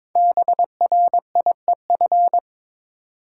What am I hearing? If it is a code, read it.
Morse code, heard as a keyed tone, about 22 wpm, 700 Hz. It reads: BRIEF
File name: 5 Letter Words - First 3 Letters Given - 22wpm